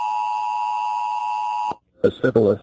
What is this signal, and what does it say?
Emergency Alert System (SAME) tones (the 853 and 960 Hz attention tone), with speech. It is heard from 0.0 to 1.8 s.